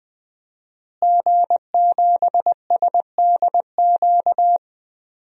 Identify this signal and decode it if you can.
Morse code, heard as a keyed tone, about 20 wpm, 700 Hz. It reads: G7SDQ
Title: DXCC-100 Call Signs - 5 Characters - Repeated 3x - 20wpm